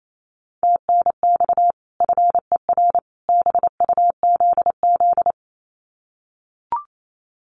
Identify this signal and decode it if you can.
Morse code, heard as a keyed tone, about 28 wpm, 700 Hz. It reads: TNX FER BUZZ
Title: Common QSO Phrases 28wpm